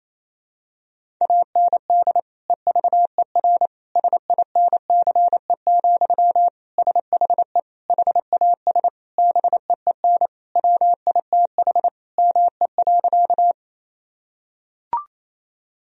Morse code, heard as a keyed tone, about 28 wpm, 700 Hz. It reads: AND EVER SINCE, SHE HAS BEEN WITH ME.
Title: Sentences from Top 1000 Words - Review Part 1 - 28wpm